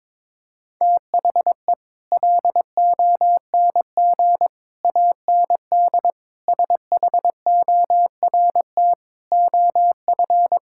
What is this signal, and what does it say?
Morse code, heard as a keyed tone, about 21 wpm, 700 Hz. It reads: THE LONG AND SHORT OF
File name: practice / 22wpm / Sets of 8 Words - 22wpm